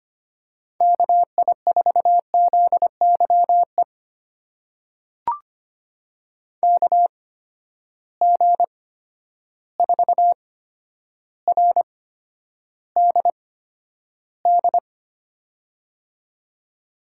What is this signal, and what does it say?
Morse code, heard as a keyed tone, about 25 wpm, spaced as for 11 wpm, 700 Hz. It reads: KI4ZYE KG4RDD
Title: USA Call Signs - 6 Characters - 8x Character Spacing - 25wpm